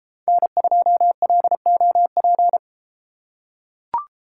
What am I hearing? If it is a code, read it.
Morse code, heard as a keyed tone, about 33 wpm, 700 Hz. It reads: N2LOP